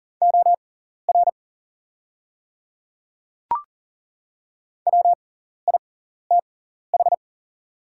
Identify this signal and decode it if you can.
Morse code, heard as a keyed tone, about 40 wpm, spaced as for 14 wpm, 700 Hz. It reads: OR WITH
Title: 6x Character Spacing - Top 100 Words - 40wpm